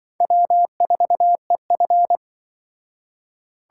Morse code, heard as a keyed tone, about 24 wpm, 700 Hz. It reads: W4EF
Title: Speed Racing - USA 1x2 Call Signs 20wpm